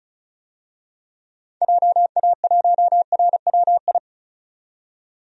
Morse code, heard as a keyed tone, about 34 wpm, 700 Hz. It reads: JA1RWI
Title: DXCC-100 Call Signs - 6 Characters - 35wpm